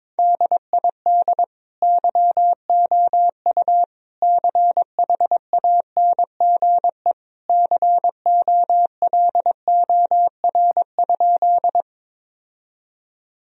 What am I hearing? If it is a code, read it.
Morse code, heard as a keyed tone, about 22 wpm, 700 Hz. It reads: DID YOU CHANGE COLOR?
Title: Sentences from Top 1000 Words - Review Part 2 - 22wpm